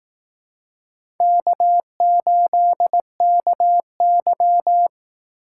Morse code, heard as a keyed tone, about 18 wpm, 700 Hz. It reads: K8KY
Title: Speed Racing - USA 1x2 Call Signs 15wpm